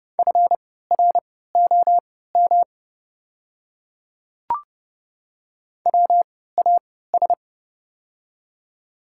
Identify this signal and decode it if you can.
Morse code, heard as a keyed tone, about 30 wpm, spaced as for 17 wpm, 700 Hz. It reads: FROM WAS
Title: 3x Character Spacing - Top 100 Words - 30wpm